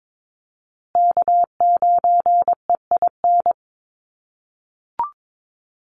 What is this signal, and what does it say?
Morse code, heard as a keyed tone, about 22 wpm, 700 Hz. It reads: K9EIN
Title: USA Call Signs Repeated 2x 22wpm